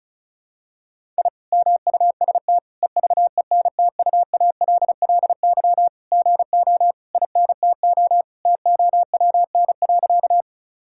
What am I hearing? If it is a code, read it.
Morse code, heard as a keyed tone, about 34 wpm, 700 Hz. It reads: I MUST EVENTUALLY GO INTO TOWN.
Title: Sentences from Top 1000 Words - Review Part 6 - 35wpm